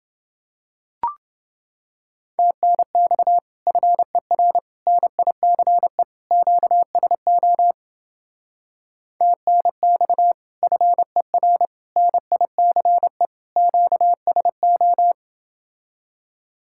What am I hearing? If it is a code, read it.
Morse code, heard as a keyed tone, about 28 wpm, 700 Hz. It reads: TNX FER NICE QSO TNX FER NICE QSO